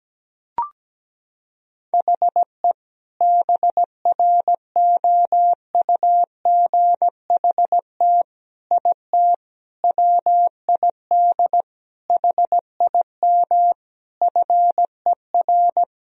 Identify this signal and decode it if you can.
Morse code, heard as a keyed tone, about 17 wpm, 700 Hz. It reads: HE BROUGHT IT WID HIM FER
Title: Sentences from Top 500 Words and Abbreviations 17wpm